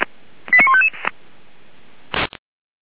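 Five-tone selective call (ZVEI), heard as quick runs of five tones, tones of about 70 ms. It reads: 89155
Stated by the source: ZVEI five-tone selective call received by SDR, ID 89236, channel 464.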